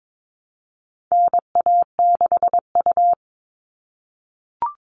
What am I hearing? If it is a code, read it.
Morse code, heard as a keyed tone, about 22 wpm, 700 Hz. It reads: NA6U
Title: All Cut Numbers - 4-Digit 22wpm